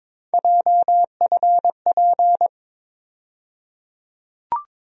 Morse code, heard as a keyed tone, about 22 wpm, 700 Hz. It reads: JFP
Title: USA Call Sign Postfixes - 3 Letters 22wpm